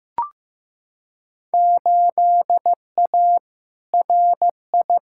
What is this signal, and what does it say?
Morse code, heard as a keyed tone, about 15 wpm, 700 Hz. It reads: 8A RI
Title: Field Day Exchanges Encore - 15wpm